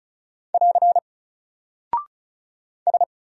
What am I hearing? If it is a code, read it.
Morse code, heard as a keyed tone, about 36 wpm, 700 Hz. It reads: <AR> S
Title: Single Character (Letter:Number:Punctuation:Prosign) Alt 35wpm